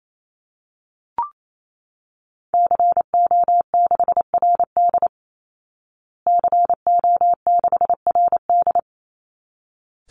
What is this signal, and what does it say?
Morse code, heard as a keyed tone, about 28 wpm, 700 Hz. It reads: CO6RD CO6RD
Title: Call Signs Repeated 3x - 28wpm